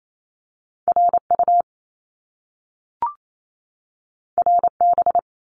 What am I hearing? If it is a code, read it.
Morse code, heard as a keyed tone, about 28 wpm, 700 Hz. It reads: RU RB
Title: Top Two Letter Combinations in Words 28wpm